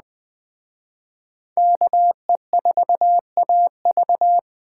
Morse code, heard as a keyed tone, about 20 wpm, 700 Hz. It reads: KE4AV